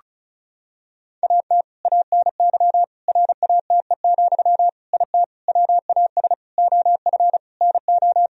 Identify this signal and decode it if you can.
Morse code, heard as a keyed tone, about 36 wpm, 700 Hz. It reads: AT ANY RATE, IT WAS OF NO
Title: Sentences from Top 1000 Words - Review Part 4 - 35wpm